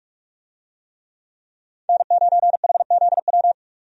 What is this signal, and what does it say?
Morse code, heard as a keyed tone, about 44 wpm, 700 Hz. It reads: N9HZW
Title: USA Call Signs - 5 Characters - 45wpm